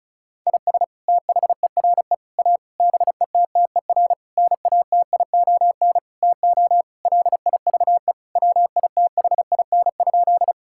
Morse code, heard as a keyed tone, about 36 wpm, 700 Hz. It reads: IS THERE A BETTER NATION TO LIVE WITHIN?
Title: Sentences from Top 1000 Words - Review Part 6 - 35wpm